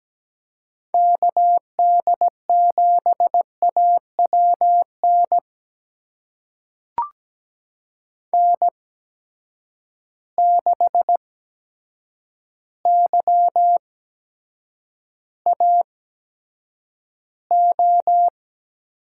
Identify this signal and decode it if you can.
Morse code, heard as a keyed tone, about 17 wpm, spaced as for 8 wpm, 700 Hz. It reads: KD7AWN N6YAO
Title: USA Call Signs - 8x Character Spacing - 17wpm